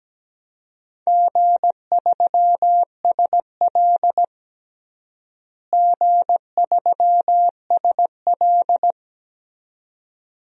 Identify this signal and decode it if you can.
Morse code, heard as a keyed tone, about 17 wpm, 700 Hz. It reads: G3SL G3SL
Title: DXCC-100 Call Signs - 4 Characters - Repeated 2x - 17wpm